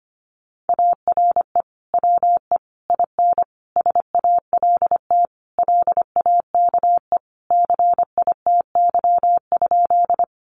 Morse code, heard as a keyed tone, about 25 wpm, 700 Hz. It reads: ARE WE IN SALT LAKE CITY?
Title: Sentences from Top 1100 Words 25wpm